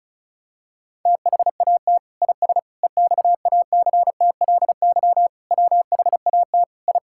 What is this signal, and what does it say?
Morse code, heard as a keyed tone, about 34 wpm, 700 Hz. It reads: THAT IS EXACTLY WHAT I
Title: Sentences from Top 1000 Words - Review Part 6 - 35wpm